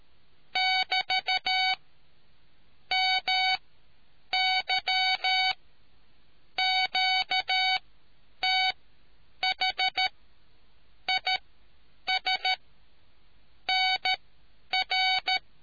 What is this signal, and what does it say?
Morse code, heard as a keyed tone, about 13 wpm, spaced as for 8 wpm, 750 Hz. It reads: <BT> MY QTH IS NR